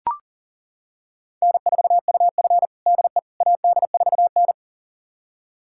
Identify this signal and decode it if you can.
Morse code, heard as a keyed tone, about 40 wpm, 700 Hz. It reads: N4UF DE AD4N
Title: USA Callsign DE Callsign - 4 Characters - 40wpm